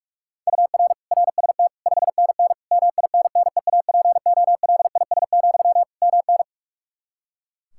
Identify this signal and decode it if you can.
Morse code, heard as a keyed tone, about 46 wpm, 700 Hz. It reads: UR RST 5NN MINNEAPOLIS, MN
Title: Practice QSO - Exchange 1 - Beginning - 45wpm